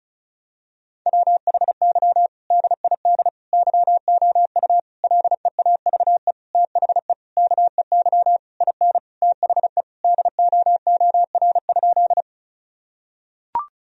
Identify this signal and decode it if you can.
Morse code, heard as a keyed tone, about 36 wpm, 700 Hz. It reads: WHY DID YOU LEAVE THE KEY IN THE DOOR?
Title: Sentences from Top 1000 Words - Review Part 4 - 35wpm